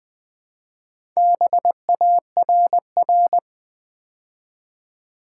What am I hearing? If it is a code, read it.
Morse code, heard as a keyed tone, about 20 wpm, 690 Hz. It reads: BARR
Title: Build A Word - 2 Letters At A Time - 6-Letter Words - 20wpm